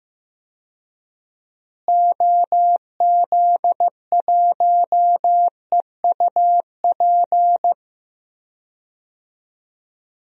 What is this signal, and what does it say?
Morse code, heard as a keyed tone, about 15 wpm, 700 Hz. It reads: OZ1EUP
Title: DXCC-100 Call Signs - 6 Characters - Repeated 2x - 15wpm